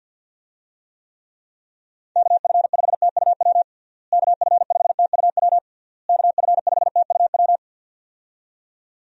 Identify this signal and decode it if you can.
Morse code, heard as a keyed tone, about 47 wpm, 700 Hz. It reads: KF5TUW KF5TUW KF5TUW
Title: USA Call Signs - 6 Characters - Repeated 3x - 50wpm